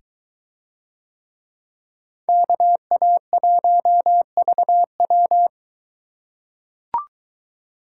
Morse code, heard as a keyed tone, about 22 wpm, 700 Hz. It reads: KA1VW